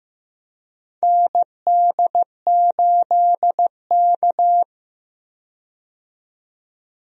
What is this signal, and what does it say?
Morse code, heard as a keyed tone, about 15 wpm, 700 Hz. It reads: ND8K
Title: USA Call Signs - 4 Characters - 15wpm